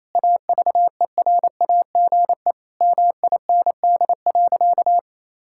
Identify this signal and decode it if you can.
Morse code, heard as a keyed tone, about 27 wpm, 700 Hz. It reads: AVERAGE MIND.
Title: Sentences from Top 1000 Words - Review Part 6 - 28wpm